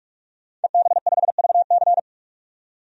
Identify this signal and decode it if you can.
Morse code, heard as a keyed tone, about 47 wpm, 700 Hz. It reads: EB5VC
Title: DXCC-100 Call Signs - 5 Characters - 45wpm